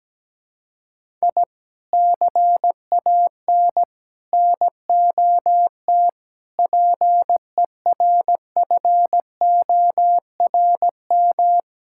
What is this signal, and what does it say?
Morse code, heard as a keyed tone, about 17 wpm, 700 Hz. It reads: I CAN NOT PERFORM
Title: Sentences from Top 1000 Words - Review Part 5 - 17wpm